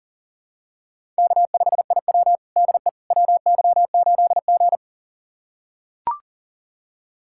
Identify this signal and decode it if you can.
Morse code, heard as a keyed tone, about 40 wpm, 700 Hz. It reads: K5IW DE WY8G
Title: USA Callsign DE Callsign - 4 Characters - 40wpm